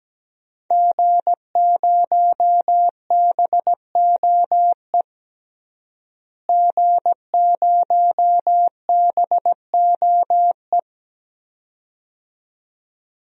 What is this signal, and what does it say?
Morse code, heard as a keyed tone, about 17 wpm, 700 Hz. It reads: G0BOE G0BOE
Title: DXCC-100 Call Signs - 5 Characters - Repeated 2x - 17wpm